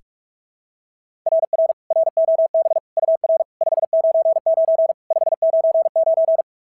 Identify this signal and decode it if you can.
Morse code, heard as a keyed tone, about 46 wpm, 690 Hz. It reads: RR ROB UR 599 599
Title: SOTA QSO Practice 45wpm